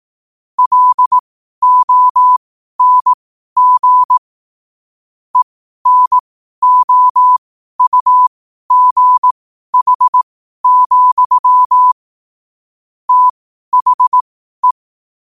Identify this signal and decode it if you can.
Morse code, heard as a keyed tone, about 18 wpm, spaced as for 12 wpm, 1000 Hz. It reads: LONG ENOUGH, THE
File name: qotdfiles / 12wpm